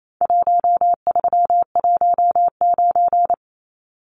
Morse code, heard as a keyed tone, about 28 wpm, 700 Hz. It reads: 1319
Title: Numbers - Contest Style - 4-Digit 28wpm